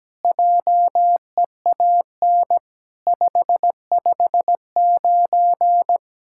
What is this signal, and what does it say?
Morse code, heard as a keyed tone, about 17 wpm, 700 Hz. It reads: JEAN 559
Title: SOTA QSO Practice 17wpm